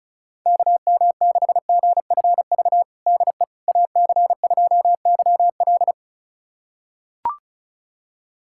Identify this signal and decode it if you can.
Morse code, heard as a keyed tone, about 36 wpm, 700 Hz. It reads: KM6GFV DE AC2YL
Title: USA Callsign DE Callsign 35wpm